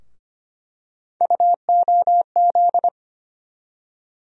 Morse code, heard as a keyed tone, about 25 wpm, 700 Hz. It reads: UOZ
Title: USA Call Sign Postfixes - 3 Letters 25wpm